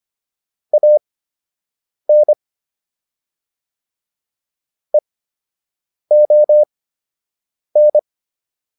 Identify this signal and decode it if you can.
Morse code, heard as a keyed tone, about 25 wpm, spaced as for 7 wpm, 600 Hz. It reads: AN EON